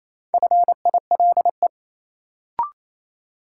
Morse code, heard as a keyed tone, about 27 wpm, 700 Hz. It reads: FILE
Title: Sentences from Top 1000 Words - Review Part 4 - New Word Drill - 28wpm